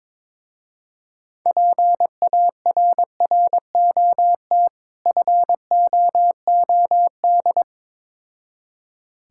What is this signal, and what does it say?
Morse code, heard as a keyed tone, about 22 wpm, 700 Hz. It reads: PARROT FOOD
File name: Difficult Words with Hints - 22wpm